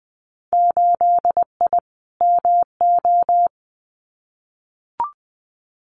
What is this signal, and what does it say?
Morse code, heard as a keyed tone, about 20 wpm, 700 Hz. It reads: 8I MO
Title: Winter Field Day Exchanges Encore 20wpm